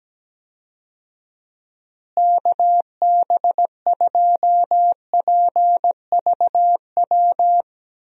Morse code, heard as a keyed tone, about 17 wpm, 700 Hz. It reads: KB2PVW
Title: USA Call Signs - 6 Characters - Repeated 2x - 17wpm